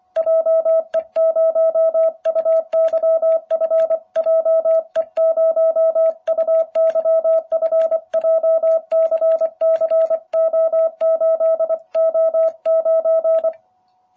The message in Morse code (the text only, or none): JE0UYF JE0UYF JCCO8O9